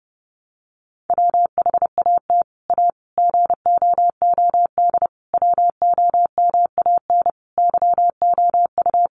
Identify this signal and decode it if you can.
Morse code, heard as a keyed tone, about 30 wpm, 700 Hz. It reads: WHAT A GOOD WOMAN YOU
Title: Sentences from Top 1100 Words 30wpm